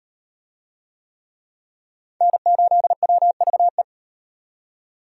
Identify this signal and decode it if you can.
Morse code, heard as a keyed tone, about 36 wpm, 700 Hz. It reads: N8WVE